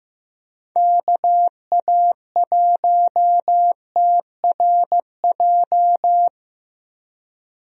Morse code, heard as a keyed tone, about 15 wpm, 700 Hz. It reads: KA1TRJ
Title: USA Call Signs - 6 Characters - Repeated 2x - 15wpm